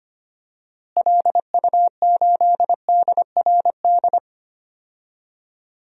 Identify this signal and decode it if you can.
Morse code, heard as a keyed tone, about 25 wpm, 700 Hz. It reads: LU8DRD
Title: DXCC-100 Call Signs - 6 Characters - Repeated 3x - 25wpm